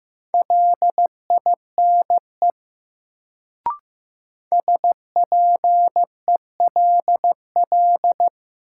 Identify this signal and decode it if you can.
Morse code, heard as a keyed tone, about 15 wpm, 700 Hz. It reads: LINE SPELL
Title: Sentences from Top 1000 Words - Review Part 1 - New Word Drill - 15wpm